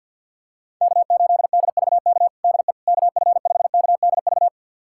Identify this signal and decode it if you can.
Morse code, heard as a keyed tone, about 50 wpm, 700 Hz. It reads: K8DVK DE KF5KDV
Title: USA Callsign DE Callsign 50wpm